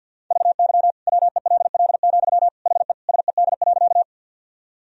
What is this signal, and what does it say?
Morse code, heard as a keyed tone, about 51 wpm, 700 Hz. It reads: VY WELL, HE SED.
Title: Sentences from Top 200 Words and Abbreviations 50wpm